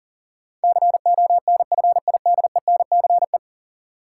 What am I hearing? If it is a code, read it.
Morse code, heard as a keyed tone, about 40 wpm, 700 Hz. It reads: CONFIDENCE
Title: Sentences from Top 1500 Words - New Word Drill - 40wpm